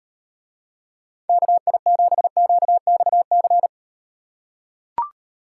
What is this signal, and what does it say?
Morse code, heard as a keyed tone, about 36 wpm, 700 Hz. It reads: KI7QXC